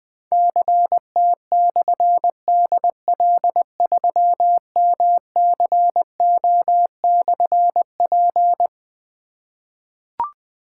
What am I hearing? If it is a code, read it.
Morse code, heard as a keyed tone, about 20 wpm, 700 Hz. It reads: CT/DL3MCO/P
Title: Difficult Call Signs 20wpm